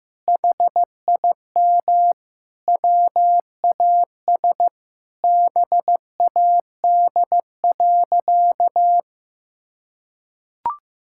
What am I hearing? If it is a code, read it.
Morse code, heard as a keyed tone, about 15 wpm, 700 Hz. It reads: HIM WAS BAD.
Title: Sentences from Top 1600 Words 15wpm